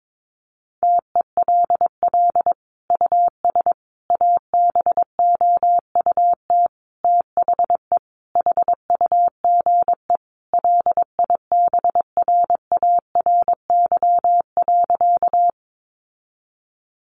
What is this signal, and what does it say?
Morse code, heard as a keyed tone, about 22 wpm, 700 Hz. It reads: TELL US ABOUT THE HUGE LIBRARY.
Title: Sentences from Top 1100 Words 22wpm